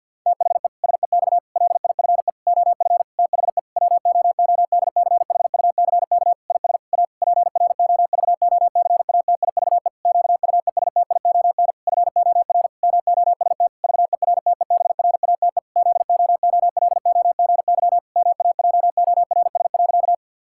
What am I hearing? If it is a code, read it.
Morse code, heard as a keyed tone, about 50 wpm, 700 Hz. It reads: THE SEX LIFE OF THE WOODCHUCK IS A PROVOCATIVE QUESTION FOR MOST VERTEBRATE ZOOLOGY MAJORS.
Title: Pangram Challenge v2 - 50wpm